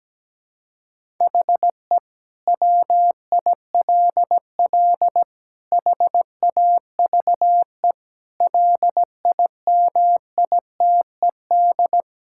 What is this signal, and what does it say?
Morse code, heard as a keyed tone, about 17 wpm, 700 Hz. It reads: HE WILL HAVE LIMITED
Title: Sentences from Top 1000 Words - Review Part 6 - 17wpm